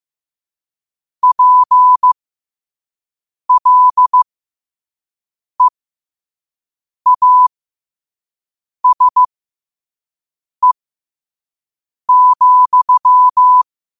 Morse code, heard as a keyed tone, about 15 wpm, spaced as for 5 wpm, 1000 Hz. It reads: PLEASE,
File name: qotdfiles / 5wpm